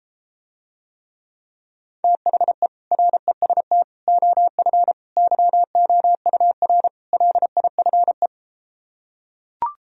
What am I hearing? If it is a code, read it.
Morse code, heard as a keyed tone, about 33 wpm, 700 Hz. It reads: THE REST OF YOUR LIFE